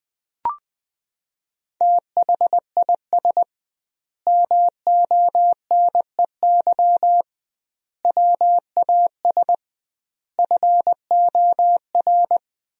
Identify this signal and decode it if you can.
Morse code, heard as a keyed tone, about 20 wpm, 700 Hz. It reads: THIS MONEY WAS FOR
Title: 2x Word-Spacing - Sentences from Top 400 Words - 20wpm